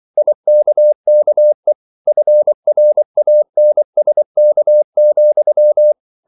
Morse code, heard as a keyed tone, about 24 wpm, 600 Hz. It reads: IKKE FRANSK,
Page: Ferie 24 wpm